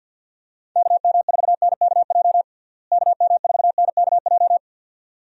Morse code, heard as a keyed tone, about 48 wpm, 700 Hz. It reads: KM4NKJ KM4NKJ
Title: USA Call Signs - 6 Characters - Repeated 2x - 50wpm